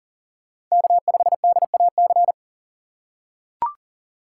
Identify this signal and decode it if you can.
Morse code, heard as a keyed tone, about 40 wpm, 700 Hz. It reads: K5DAC
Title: USA Call Signs - 5 Characters - 4x Character Spacing - 40wpm